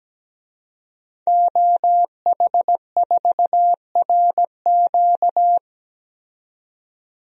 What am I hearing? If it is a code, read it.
Morse code, heard as a keyed tone, about 17 wpm, 700 Hz. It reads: OH4RQ
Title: DXCC-100 Call Signs - 5 Characters - 17wpm